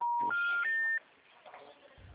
Special information tone.